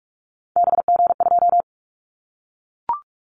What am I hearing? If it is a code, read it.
Morse code, heard as a keyed tone, about 44 wpm, 700 Hz. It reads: BG2
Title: Callsign Prefixes for Top 100 DXCC - 45wpm